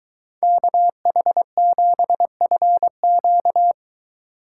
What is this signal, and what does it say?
Morse code, heard as a keyed tone, about 23 wpm, 700 Hz. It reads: KH7FQ